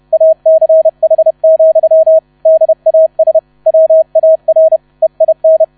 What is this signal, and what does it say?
Morse code, heard as a keyed tone, about 31 wpm, 640 Hz, with radio noise. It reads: ACH, DAS WAR EIN
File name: 30wpm